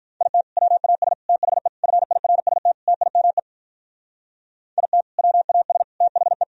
Morse code, heard as a keyed tone, about 43 wpm, 700 Hz. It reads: IT WAS THE FIRST TIME IT WAS THE
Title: Speed-Racing - Sets of 5 Words 35wpm